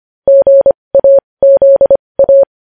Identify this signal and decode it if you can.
Morse code, heard as a keyed tone, about 25 wpm, spaced as for 20 wpm, 550 Hz. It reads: GAZA
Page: news-headlines-in-morse-code-at-20-wpm